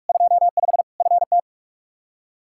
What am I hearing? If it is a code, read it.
Morse code, heard as a keyed tone, about 41 wpm, 700 Hz. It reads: 25 FT
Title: Practice QSO - Exchange 2 - Info - 45wpm